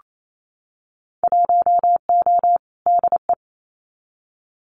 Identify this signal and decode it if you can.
Morse code, heard as a keyed tone, about 27 wpm, 700 Hz. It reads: 1O DE
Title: Winter Field Day Exchanges 28wpm